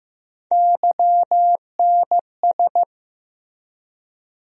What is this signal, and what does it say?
Morse code, heard as a keyed tone, about 15 wpm, 700 Hz. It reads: YNS
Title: USA Call Sign Postfixes - 3 Letters 15wpm